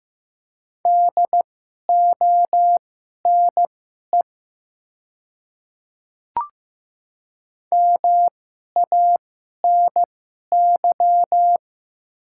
Morse code, heard as a keyed tone, about 15 wpm, spaced as for 11 wpm, 690 Hz. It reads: DONE MANY
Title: 2x Character Spacing - Top 500 Words - 15wpm